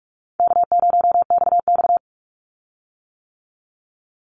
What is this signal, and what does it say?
Morse code, heard as a keyed tone, about 46 wpm, 700 Hz. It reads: K0XX
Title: Easy Call Signs - 45wpm